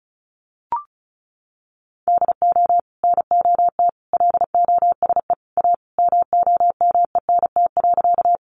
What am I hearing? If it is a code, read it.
Morse code, heard as a keyed tone, about 34 wpm, 700 Hz. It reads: DO NOT LOSE A MOMENT.
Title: Sentences from Top 1100 Words 35wpm